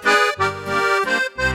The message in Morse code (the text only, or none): none